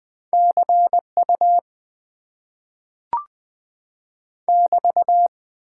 Morse code, heard as a keyed tone, about 20 wpm, 700 Hz. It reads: CU <BT>
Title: Top 90 QSO Elements - 8x Character Spacing - 20wpm